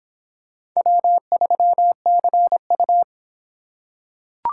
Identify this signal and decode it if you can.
Morse code, heard as a keyed tone, about 24 wpm, 700 Hz. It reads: W3CU